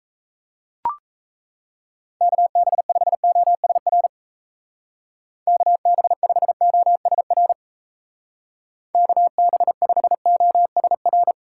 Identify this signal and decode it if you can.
Morse code, heard as a keyed tone, about 34 wpm, 700 Hz. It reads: KB5OSR KB5OSR KB5OSR
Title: Speed Racing - USA 2x3 Call Signs 28wpm